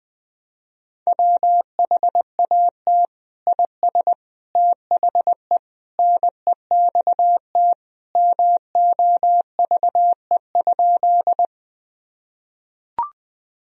Morse code, heard as a keyed tone, about 21 wpm, 700 Hz. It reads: WHAT IS THE NEXT MOVE?
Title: Sentences from Top 1000 Words - Review Part 1 - 20wpm